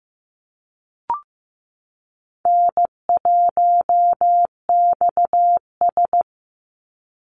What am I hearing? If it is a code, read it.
Morse code, heard as a keyed tone, about 15 wpm, 700 Hz. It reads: N1XS
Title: Easy Call Signs Repeated 3x - 15wpm